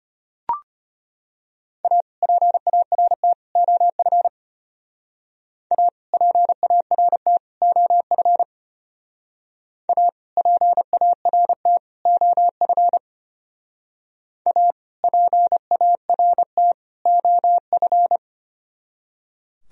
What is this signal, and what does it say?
Morse code, heard as a keyed tone, about 31 wpm, 700 Hz. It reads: A PART OF A PART OF A PART OF A PART OF